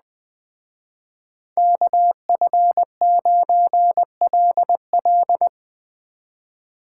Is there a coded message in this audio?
Yes: Morse code from 1.6 to 5.5 s, heard as a keyed tone, about 20 wpm, 700 Hz. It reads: KF9LL